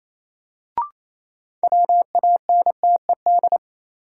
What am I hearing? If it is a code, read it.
Morse code, heard as a keyed tone, about 28 wpm, 700 Hz. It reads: WANTED
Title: Sets of 2 Words - New Word Drill - 28wpm